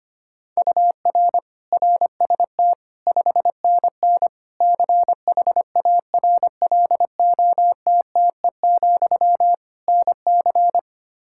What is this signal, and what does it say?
Morse code, heard as a keyed tone, about 25 wpm, 700 Hz. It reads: UR RST 5NN CHARLOTTE, NC
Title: K1USN SST QSO Practice 25wpm